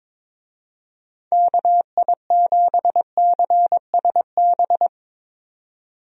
Morse code, heard as a keyed tone, about 22 wpm, 700 Hz. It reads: KI7CSB
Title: USA Call Signs - 6 Characters - 22wpm